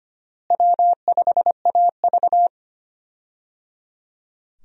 Morse code, heard as a keyed tone, about 25 wpm, 700 Hz. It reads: W5AV
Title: USA Call Signs - 4 Characters - Repeated 3x - 25wpm